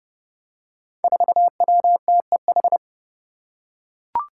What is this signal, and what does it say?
Morse code, heard as a keyed tone, about 30 wpm, 700 Hz. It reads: 4WTEH
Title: Five Challenging Letter-Number Combinations 30wpm